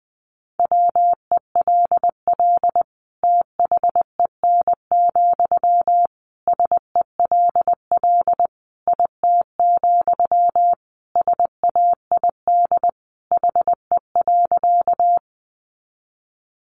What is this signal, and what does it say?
Morse code, heard as a keyed tone, about 20 wpm, 700 Hz. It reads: WELL THEN, SELL IT, SAID HE.
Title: Sentences from Top 1100 Words 20wpm